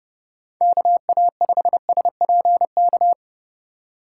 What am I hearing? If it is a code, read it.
Morse code, heard as a keyed tone, about 34 wpm, 700 Hz. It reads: KA5SPK